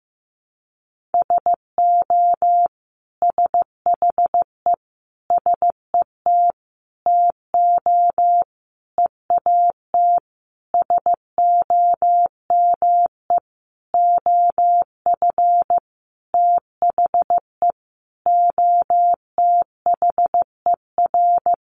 Morse code, heard as a keyed tone, about 15 wpm, 700 Hz. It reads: SO SHE SET TO EAT SOME OF THE OTHER
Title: Sentences from Top 900 Words 15wpm